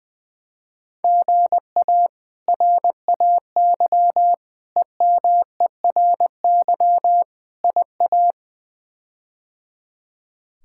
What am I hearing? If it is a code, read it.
Morse code, heard as a keyed tone, about 20 wpm, 700 Hz. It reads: GA RAY EMERY IA
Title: K1USN SST QSO Practice - 20wpm